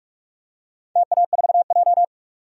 Morse code, heard as a keyed tone, about 46 wpm, 700 Hz. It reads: TA4J
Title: DXCC-100 Call Signs - 4 Characters - 45wpm